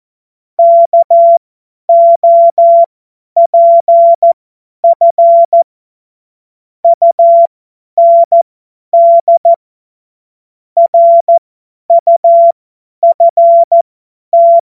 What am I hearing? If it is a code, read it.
Morse code, heard as a keyed tone, about 14 wpm, spaced as for 10 wpm, 680 Hz. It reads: KOPF UND RUFT
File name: Morsekurs / Witze / Witz 18. Januar 2022, 50 F70